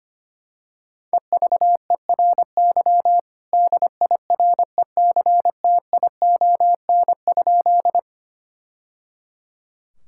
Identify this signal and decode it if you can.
Morse code, heard as a keyed tone, about 25 wpm, 700 Hz. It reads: EVERY DIRECTION?
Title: Sentences from Top 1000 Words - Review Part 5 - 25wpm